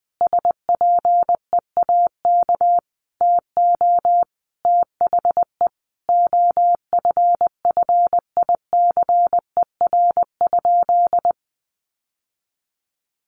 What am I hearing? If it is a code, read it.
Morse code, heard as a keyed tone, about 20 wpm, 700 Hz. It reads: SPEAK TO THE OFFICER?
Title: Sentences from Top 1200 Words 20wpm